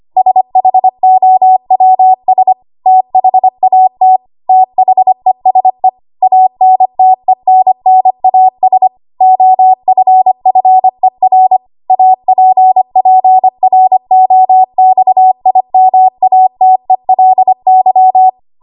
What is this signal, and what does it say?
Morse code, heard as a keyed tone, about 25 wpm, 750 Hz. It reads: SHOWS THAT THESE ANTENNAS OFFER APPROXIMATELY
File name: Archive / 25 WPM / 25WPM